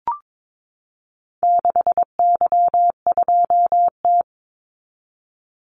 Morse code, heard as a keyed tone, about 22 wpm, 700 Hz. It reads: 6Y2T
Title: Call Signs Repeated 3x - 22wpm